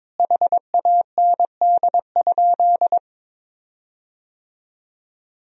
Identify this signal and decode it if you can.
Morse code, heard as a keyed tone, about 22 wpm, 700 Hz. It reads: HAND?
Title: 4x Word-Spacing - Sentences from Top 200 Words - 22wpm